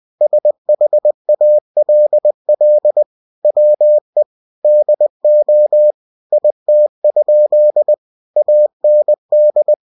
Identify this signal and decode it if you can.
Morse code, heard as a keyed tone, about 20 wpm, 600 Hz. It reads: SHALL WE DO IT? AND
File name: cw-books / winnie-the-pooh / winnie-20wpm-page-0034